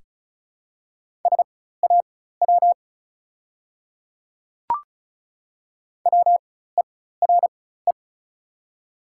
Morse code, heard as a keyed tone, about 36 wpm, spaced as for 16 wpm, 700 Hz. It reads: SAW WERE
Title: 4x Character Spacing - Top 300 Words - 35wpm